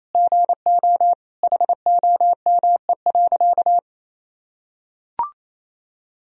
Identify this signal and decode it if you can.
Morse code, heard as a keyed tone, about 28 wpm, 700 Hz. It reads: GO HOME.
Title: Sentences from Top 200 Words and Abbreviations 28wpm